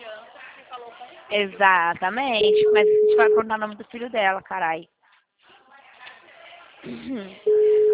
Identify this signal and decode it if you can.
Ringback tone (Europe, 425 Hz), with speech.